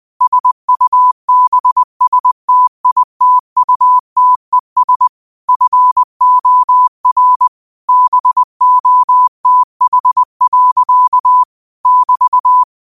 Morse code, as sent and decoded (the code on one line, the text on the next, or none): ... ..- -... ... - .. - ..- - . ... / ..-. --- .-. / -... --- - .... .-.-.- / -...-
SUBSTITUTES FOR BOTH. <BT>